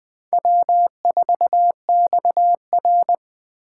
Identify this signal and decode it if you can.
Morse code, heard as a keyed tone, about 20 wpm, 700 Hz. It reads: W4XR
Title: USA Call Signs - 2x Character Spacing - 20wpm